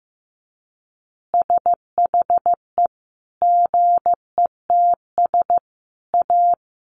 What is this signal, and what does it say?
Morse code, heard as a keyed tone, about 15 wpm, 700 Hz. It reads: SHE GETS A
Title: Additional Easy Phrases 15wpm